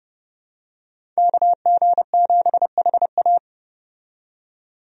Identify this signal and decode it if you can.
Morse code, heard as a keyed tone, about 30 wpm, 700 Hz. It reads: KG7HA
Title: USA Call Signs - 5 Characters - 30wpm